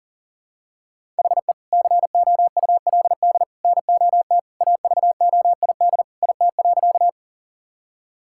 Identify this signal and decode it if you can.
Morse code, heard as a keyed tone, about 40 wpm, 700 Hz. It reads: HE COULD NOT AVOID IT.
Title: Sentences from Top 1000 Words - Review Part 6 - 40wpm